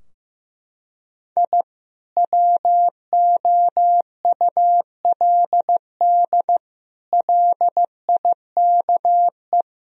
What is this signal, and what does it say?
Morse code, heard as a keyed tone, about 15 wpm, 700 Hz. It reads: I WOULD LIKE
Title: Sentences from Top 1000 Words - Review Part 4 - 15wpm